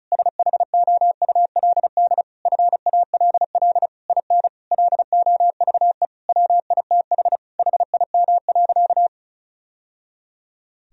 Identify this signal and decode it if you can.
Morse code, heard as a keyed tone, about 36 wpm, 700 Hz. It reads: SHOULD FALL IN LOVE WITH HIM.
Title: Sentences from Top 1000 Words - Review Part 4 - 35wpm